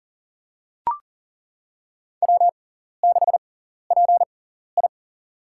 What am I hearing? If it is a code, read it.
Morse code, heard as a keyed tone, about 40 wpm, spaced as for 14 wpm, 700 Hz. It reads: W6PI
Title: USA Call Signs - 6x Character Spacing - 40wpm